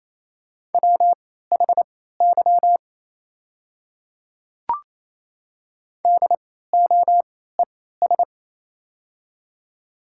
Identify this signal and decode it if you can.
Morse code, heard as a keyed tone, about 28 wpm, spaced as for 16 wpm, 700 Hz. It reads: WHY DOES
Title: 3x Character Spacing - Top 200 Words - 28wpm